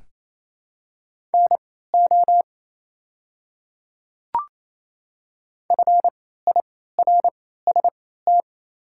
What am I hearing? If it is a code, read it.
Morse code, heard as a keyed tone, about 27 wpm, spaced as for 16 wpm, 690 Hz. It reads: NO FIRST
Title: 3x Character Spacing - Top 200 Words - 28wpm